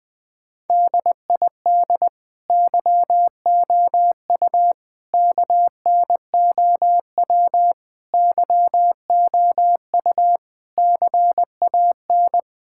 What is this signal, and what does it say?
Morse code, heard as a keyed tone, about 21 wpm, 700 Hz. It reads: DID YOU KNOW YOU CAN
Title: Sentences with 9-Word Phrases - 20wpm